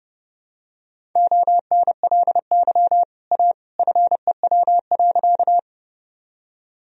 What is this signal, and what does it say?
Morse code, heard as a keyed tone, about 30 wpm, 700 Hz. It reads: ONLY A FEW.
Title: Sentences from Top 1000 Words - Review Part 1 - 30wpm